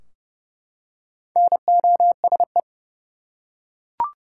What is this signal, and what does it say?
Morse code, heard as a keyed tone, about 30 wpm, 700 Hz. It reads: NOSE
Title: Four Letter Words - 30wpm